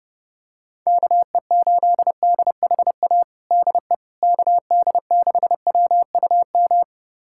Morse code, heard as a keyed tone, about 30 wpm, 700 Hz. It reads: KE8DHA DE KD6WUM
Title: USA Callsign DE Callsign - 6 Characters - 30wpm